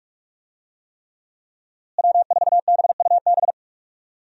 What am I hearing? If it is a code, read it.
Morse code, heard as a keyed tone, about 46 wpm, 700 Hz. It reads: W4BUB
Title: USA Call Signs - 5 Characters - Repeated 3x - 45wpm